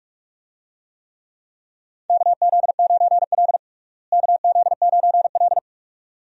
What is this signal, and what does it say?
Morse code, heard as a keyed tone, about 45 wpm, 700 Hz. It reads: KZ9L KZ9L
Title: USA Call Signs - 4 Characters - Repeated 2x - 45wpm